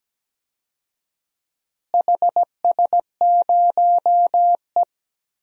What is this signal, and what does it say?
Morse code, heard as a keyed tone, about 17 wpm, 700 Hz. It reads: HS0E
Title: DXCC-100 Call Signs - 4 Characters - Repeated 3x - 17wpm